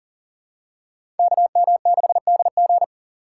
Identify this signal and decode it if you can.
Morse code, heard as a keyed tone, about 40 wpm, 700 Hz. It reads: KM6DG
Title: USA Call Signs - 5 Characters - 2x Character Spacing - 40wpm